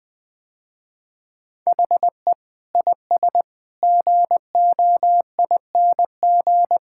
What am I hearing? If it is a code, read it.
Morse code, heard as a keyed tone, about 20 wpm, 700 Hz. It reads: HE IS GOING
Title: Sentences with 2-Word Phrases - Spoken Beginning - 20wpm